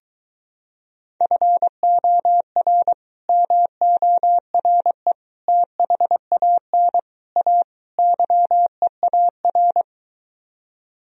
Morse code, heard as a keyed tone, about 22 wpm, 700 Hz. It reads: FOR MORE THAN A YEAR